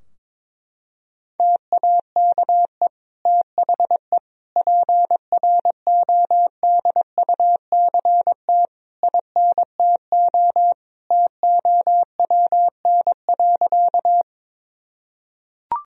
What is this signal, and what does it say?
Morse code, heard as a keyed tone, about 22 wpm, 700 Hz. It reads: TAKE THE PRODUCT INTO TOWN.
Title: Sentences from Top 1000 Words - Review Part 2 - 22wpm